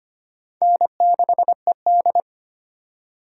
Morse code, heard as a keyed tone, about 25 wpm, 700 Hz. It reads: N6ED
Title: USA Call Signs - 4 Characters - 4x Character Spacing - 25wpm